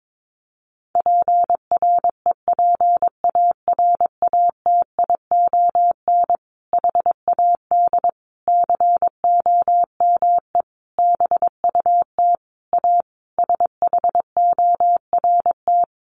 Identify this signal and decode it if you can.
Morse code, heard as a keyed tone, about 22 wpm, 700 Hz. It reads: PREPARATION HAD COME BUT A SHORT
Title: Sentences from Top 1400 Words 22wpm